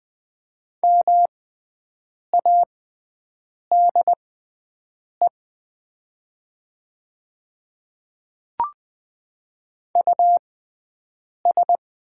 Morse code, heard as a keyed tone, about 20 wpm, spaced as for 7 wpm, 700 Hz. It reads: MADE US
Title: 6x Character Spacing - Top 500 Words - 20wpm